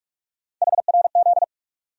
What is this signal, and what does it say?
Morse code, heard as a keyed tone, about 47 wpm, 700 Hz. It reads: HRZ
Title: USA Call Sign Postfixes - 3 Letters 45wpm